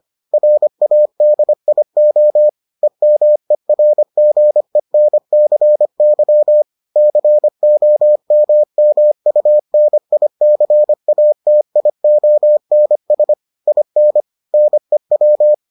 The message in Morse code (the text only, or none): RADIO EMERGENCY COMMUNICATIONS IN NEW